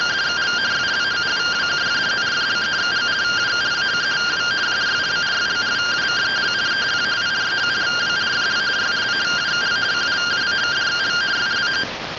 Two-tone teleprinter signal, about 170 Hz shift.